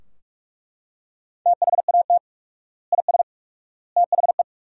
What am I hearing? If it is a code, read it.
Morse code, heard as a keyed tone, about 41 wpm, 700 Hz. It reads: THAT IS THE
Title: 4x Word-Spacing - Sentences from Top 300 Words - 45wpm